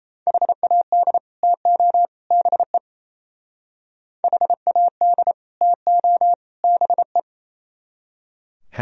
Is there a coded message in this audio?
Yes: Morse code from 0.3 to 7.2 s, heard as a keyed tone, about 37 wpm, 700 Hz. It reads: HAD TO BE HAD TO BE